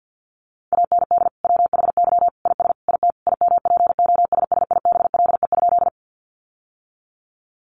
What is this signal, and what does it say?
Morse code, heard as a keyed tone, about 50 wpm, 700 Hz. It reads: AND WHY IS IT IMPOSSIBLE?
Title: Sentences from Top 1100 Words 50wpm